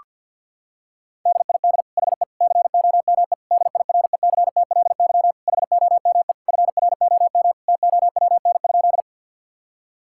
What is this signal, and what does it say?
Morse code, heard as a keyed tone, about 50 wpm, 700 Hz. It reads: DID HE COME DIRECTLY HOME FROM TOWN?
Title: Sentences from Top 1000 Words - Review Part 5 - 50wpm